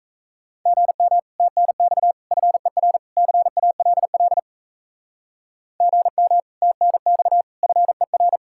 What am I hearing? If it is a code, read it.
Morse code, heard as a keyed tone, about 37 wpm, 700 Hz. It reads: GM TNX FER CALL GM TNX FER